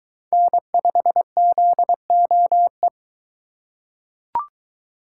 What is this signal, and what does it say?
Morse code, heard as a keyed tone, about 22 wpm, 700 Hz. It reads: N5ZOE